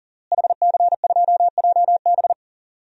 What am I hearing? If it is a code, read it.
Morse code, heard as a keyed tone, about 40 wpm, 700 Hz. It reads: HC2JB
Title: DXCC-100 Call Signs - 5 Characters - Repeated 2x - 40wpm